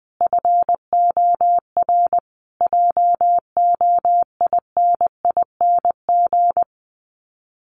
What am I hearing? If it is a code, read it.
Morse code, heard as a keyed tone, about 20 wpm, 700 Hz. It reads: FOR JOINING
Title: Sets of 5 Words Encore2 20wpm